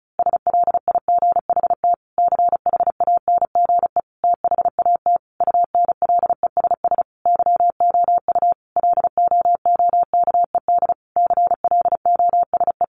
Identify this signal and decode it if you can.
Morse code, heard as a keyed tone, about 34 wpm, 700 Hz. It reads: SLIGHT CHANGE THAT UNLESS YOU LOOKED CLOSE
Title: A Sound Of Thunder 35wpm